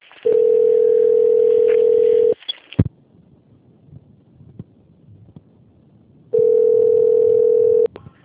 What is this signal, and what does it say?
Ringback tone (North America), 440 + 480 Hz.
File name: out- - - - - .28732